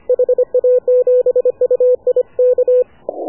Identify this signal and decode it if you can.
Morse code, heard as a keyed tone, about 26 wpm, 500 Hz. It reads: HA7UI K
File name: AUDIO2019 / HA7UI CW